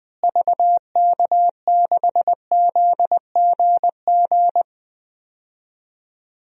Morse code, heard as a keyed tone, about 21 wpm, 700 Hz. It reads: VK6ZGG
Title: DXCC-100 Call Signs - 6 Characters - Repeated 2x - 20wpm